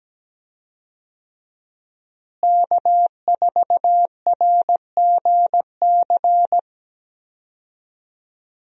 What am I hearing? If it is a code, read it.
Morse code, heard as a keyed tone, about 17 wpm, 700 Hz. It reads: K4RGC
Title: USA Call Signs - 5 Characters - Repeated 2x - 17wpm